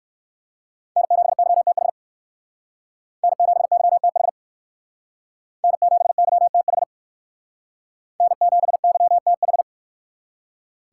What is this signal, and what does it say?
Morse code, heard as a keyed tone, about 59 wpm, 700 Hz. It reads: N7YTH N7YTH N7YTH N7YTH